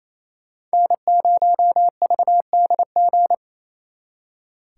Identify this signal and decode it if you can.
Morse code, heard as a keyed tone, about 34 wpm, 700 Hz. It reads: N0VDG
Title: Speed Racing - USA 1x3 Call Signs 28wpm